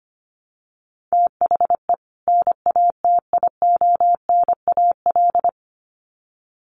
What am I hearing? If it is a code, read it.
Morse code, heard as a keyed tone, about 25 wpm, 700 Hz. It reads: THE NATIONAL
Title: Sets of 2 Words 25wpm